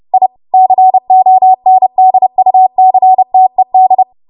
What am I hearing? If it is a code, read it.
Morse code, heard as a keyed tone, about 30 wpm, 750 Hz. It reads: I CONDUCTED
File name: Morse / 30WPM